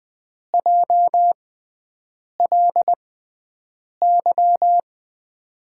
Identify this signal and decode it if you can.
Morse code, heard as a keyed tone, about 20 wpm, spaced as for 7 wpm, 700 Hz. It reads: JLY
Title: USA Call Signs - 5 Characters - 6x Character Spacing - 20wpm